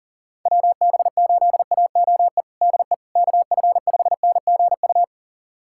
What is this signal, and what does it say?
Morse code, heard as a keyed tone, about 40 wpm, 700 Hz. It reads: WB8AOE DE KF5NGU
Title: USA Callsign DE Callsign - 6 Characters - 40wpm